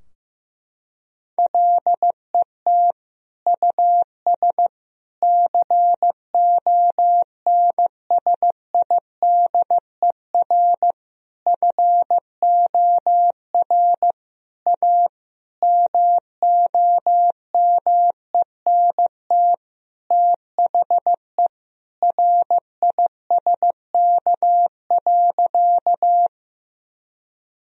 Morse code, heard as a keyed tone, about 16 wpm, 700 Hz. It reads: LET US CONSIDER FOR A MOMENT THE RISK.
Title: Sentences from Top 1000 Words - Review Part 4 - 15wpm